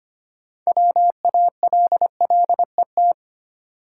Morse code, heard as a keyed tone, about 25 wpm, 690 Hz. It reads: WALLET
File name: practice / 25wpm / Easy Words with Hints - 25wpm